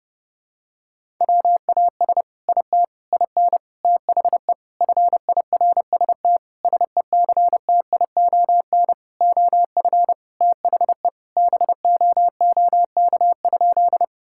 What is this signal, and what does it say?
Morse code, heard as a keyed tone, about 32 wpm, 700 Hz. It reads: WAS IT IN THE FIRST SECTION OF THE BOOK?
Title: Sentences from Top 1000 Words - Review Part 5 - 30wpm